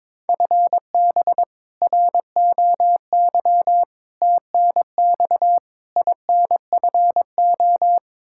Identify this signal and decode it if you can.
Morse code, heard as a keyed tone, about 22 wpm, 700 Hz. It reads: FB ROY TNX INFO
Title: K1USN SST QSO Practice 22wpm